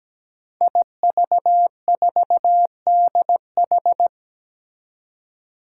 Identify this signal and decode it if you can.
Morse code, heard as a keyed tone, about 17 wpm, 700 Hz. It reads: IV4DH
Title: Five Challenging Letter-Number Combinations 17wpm